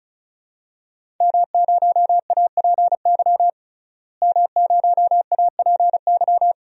Morse code, heard as a keyed tone, about 34 wpm, 700 Hz. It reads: M0APY M0APY
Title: DXCC-100 Call Signs - 5 Characters - Repeated 2x - 35wpm